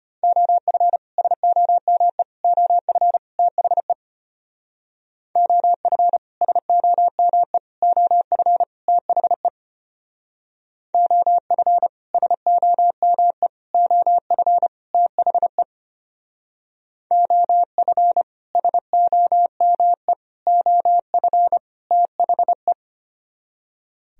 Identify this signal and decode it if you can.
Morse code, heard as a keyed tone, about 31 wpm, 700 Hz. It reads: OF SOME OF THE OF SOME OF THE OF SOME OF THE OF SOME OF THE
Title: Speed-Racing - Sets of 4 Words 25wpm